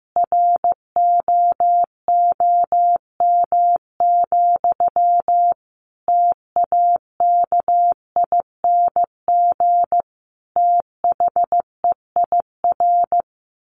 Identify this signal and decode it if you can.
Morse code, heard as a keyed tone, about 15 wpm, 700 Hz. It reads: ROOM, TAKING THEIR
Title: A Sound Of Thunder 15wpm